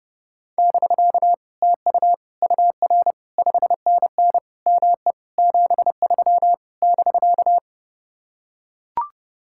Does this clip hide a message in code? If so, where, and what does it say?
Yes: Morse code from 0.6 to 7.6 s, heard as a keyed tone, about 30 wpm, 700 Hz. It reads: <BK> TU UR 5NN ME 73 <BK>